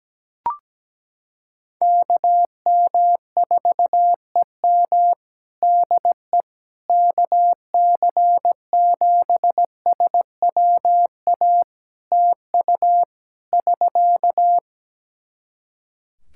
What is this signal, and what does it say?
Morse code, heard as a keyed tone, about 17 wpm, 700 Hz. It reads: KM4EM DE KC7SWA TU <SK>
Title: Practice QSO - Exchange 4 - Ending - 17wpm